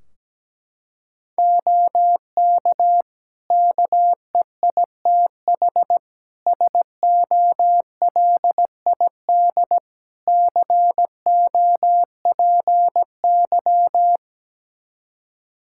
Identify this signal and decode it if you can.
Morse code, heard as a keyed tone, about 17 wpm, 700 Hz. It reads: OK KEITH SOLID COPY
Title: Practice QSO - Exchange 3 - More Info - 17wpm